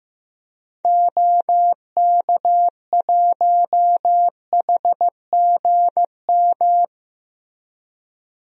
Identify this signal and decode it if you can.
Morse code, heard as a keyed tone, about 15 wpm, 700 Hz. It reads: OK1HGM
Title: DXCC-100 Call Signs - 6 Characters - Repeated 2x - 15wpm